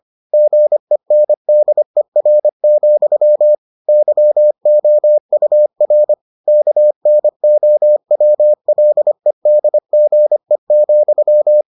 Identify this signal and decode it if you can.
Morse code, heard as a keyed tone, about 25 wpm, 600 Hz. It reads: GENDER, YOUR KNOWLEDGE,